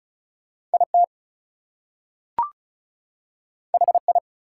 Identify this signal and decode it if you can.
Morse code, heard as a keyed tone, about 36 wpm, 700 Hz. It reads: IT HI
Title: Two Letter Words - 35wpm